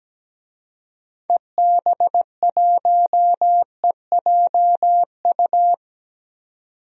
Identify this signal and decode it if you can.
Morse code, heard as a keyed tone, about 17 wpm, 700 Hz. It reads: EB1EJU
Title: DXCC-100 Call Signs - 6 Characters - 17wpm